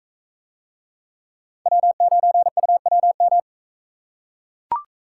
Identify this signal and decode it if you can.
Morse code, heard as a keyed tone, about 44 wpm, 700 Hz. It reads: W9UWM